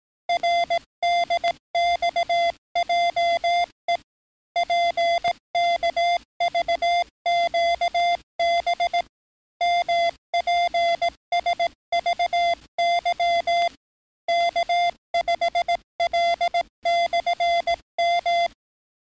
Morse code, heard as a keyed tone, about 18 wpm, 690 Hz. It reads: RDXJE PKVQB MPSVY K5L/M
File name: training / NAVPERS 11049to11060 / NAVPERS 11057 RA RB